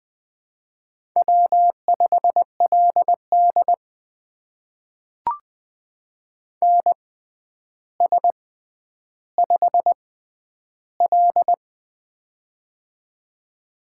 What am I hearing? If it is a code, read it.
Morse code, heard as a keyed tone, about 20 wpm, spaced as for 10 wpm, 700 Hz. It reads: W5LD NS5L
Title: USA Call Signs - 4 Characters - 6x Character Spacing - 20wpm